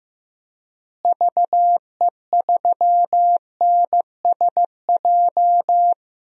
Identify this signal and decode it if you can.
Morse code, heard as a keyed tone, about 15 wpm, 700 Hz. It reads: VE3NSJ
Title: DXCC-100 Call Signs - 6 Characters - Repeated 3x - 15wpm